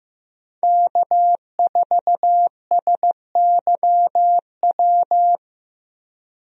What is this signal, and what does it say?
Morse code, heard as a keyed tone, about 15 wpm, 700 Hz. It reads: K4SYW
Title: USA Call Signs - 5 Characters - Repeated 3x - 15wpm